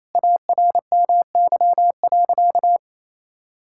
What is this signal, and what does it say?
Morse code, heard as a keyed tone, about 28 wpm, 700 Hz. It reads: ARMY.
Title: Sentences from Top 1000 Words - Review Part 6 - 28wpm